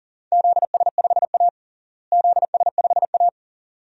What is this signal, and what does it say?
Morse code, heard as a keyed tone, about 40 wpm, 700 Hz. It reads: ZS5A ZS5A
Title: DXCC-100 Call Signs - 4 Characters - Repeated 2x - 40wpm